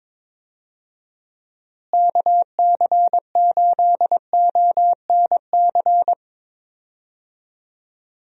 Morse code, heard as a keyed tone, about 22 wpm, 700 Hz. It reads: KC8ONC
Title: USA Call Signs - 6 Characters - Repeated 3x - 22wpm